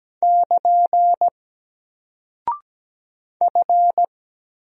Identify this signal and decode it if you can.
Morse code, heard as a keyed tone, about 17 wpm, 700 Hz. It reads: <KN> F
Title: Single Character (Letter:Number:Punctuation:Prosign) 17wpm